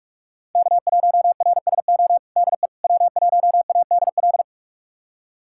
Morse code, heard as a keyed tone, about 44 wpm, 700 Hz. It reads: K1RSO DE W1ADL
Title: USA Callsign DE Callsign - 5 Characters - 45wpm